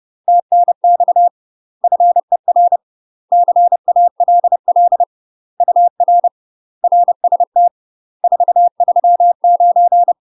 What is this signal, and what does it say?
Morse code, heard as a keyed tone, about 30 wpm, 700 Hz. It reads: TNX FER CALL UR RST 439
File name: UK MASTER QSOs TRAINING / 30WPM / UK MASTER QSOs 0008